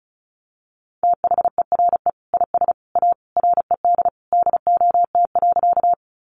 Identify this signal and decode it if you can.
Morse code, heard as a keyed tone, about 34 wpm, 700 Hz. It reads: THERE IS A RED DOT.
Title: Sentences from Top 1500 Words 35wpm